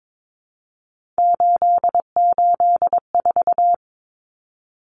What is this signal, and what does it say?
Morse code, heard as a keyed tone, about 22 wpm, 700 Hz. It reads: 884
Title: Common Cut Numbers - 3-Digit 22wpm